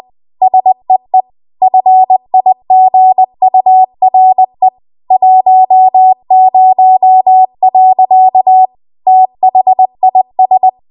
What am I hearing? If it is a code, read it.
Morse code, heard as a keyed tone, about 20 wpm, 750 Hz. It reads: SEE FIGURE 10. THIS